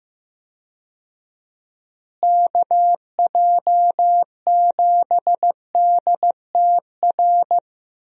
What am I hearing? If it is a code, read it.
Morse code, heard as a keyed tone, about 15 wpm, 700 Hz. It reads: KJ7DTR
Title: USA Call Signs - 6 Characters - Repeated 3x - 15wpm